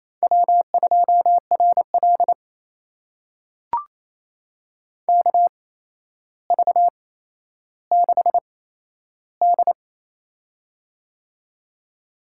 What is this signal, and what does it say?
Morse code, heard as a keyed tone, about 28 wpm, spaced as for 12 wpm, 700 Hz. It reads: W2RL KV6D
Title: USA Call Signs - 4 Characters - 8x Character Spacing - 28wpm